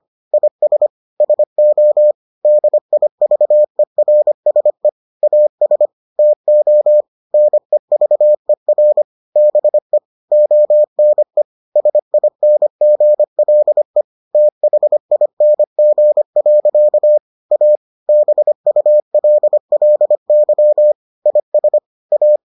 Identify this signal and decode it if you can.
Morse code, heard as a keyed tone, about 25 wpm, 600 Hz. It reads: IS SO DIVERSE AS TO NEVER BE ONE SINGLE THING. A BULLY IS A